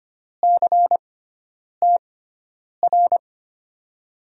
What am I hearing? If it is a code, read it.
Morse code, heard as a keyed tone, about 25 wpm, spaced as for 9 wpm, 700 Hz. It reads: CTR
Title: USA Call Signs - 6x Character Spacing - 25wpm